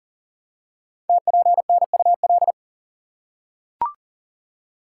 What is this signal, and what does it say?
Morse code, heard as a keyed tone, about 42 wpm, 690 Hz. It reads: TPNUL
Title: Code Groups - 5 Letters - 40wpm